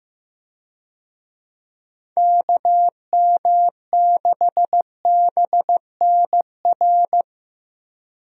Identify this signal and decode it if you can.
Morse code, heard as a keyed tone, about 15 wpm, 700 Hz. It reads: KM6BNR
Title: USA Call Signs - 6 Characters - Repeated 3x - 15wpm